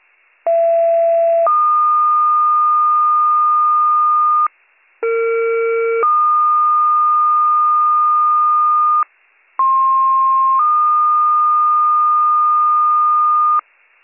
Two-tone page, heard as a one-second tone, then a three-second tone, about 670 then 1220 Hz, about 460 then 1220 Hz, about 1060 then 1220 Hz. It is heard from 0.5 to 13.6 s.